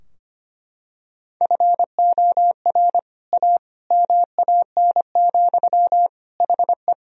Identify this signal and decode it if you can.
Morse code, heard as a keyed tone, about 25 wpm, 700 Hz. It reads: FOR A MAN, HE
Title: Sentences from Top 1000 Words - Review Part 2 - 25wpm